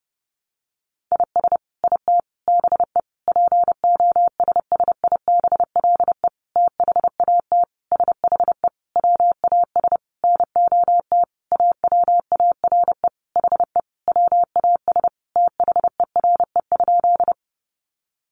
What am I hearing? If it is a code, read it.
Morse code, heard as a keyed tone, about 30 wpm, 700 Hz. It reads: IS IT BE POSSIBLE THAT SHE WAS NOT AWARE HE WAS THERE?
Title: Sentences from Top 1200 Words 30wpm